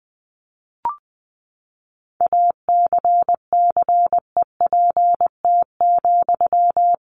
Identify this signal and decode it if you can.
Morse code, heard as a keyed tone, about 19 wpm, 700 Hz. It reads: ACCEPT,
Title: Sentences from Top 1300 Words 20wpm